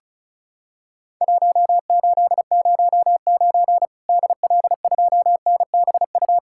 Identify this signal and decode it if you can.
Morse code, heard as a keyed tone, about 36 wpm, 700 Hz. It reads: 1809 DL2NBU
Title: Worked All Europe DX Contest - QTC Exchange - 35wpm